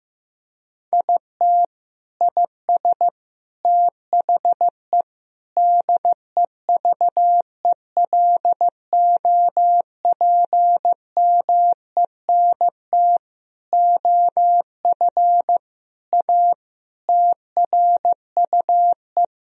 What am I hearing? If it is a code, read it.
Morse code, heard as a keyed tone, about 15 wpm, 700 Hz. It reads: IT IS THE DEVELOPMENT OF A TRUE
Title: Sentences from Top 1000 Words - Review Part 4 - 15wpm